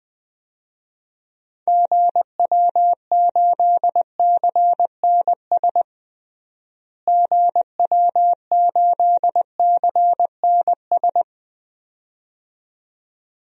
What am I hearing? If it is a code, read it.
Morse code, heard as a keyed tone, about 20 wpm, 700 Hz. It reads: GW8CNS GW8CNS
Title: DXCC-100 Call Signs - 6 Characters - Repeated 2x - 20wpm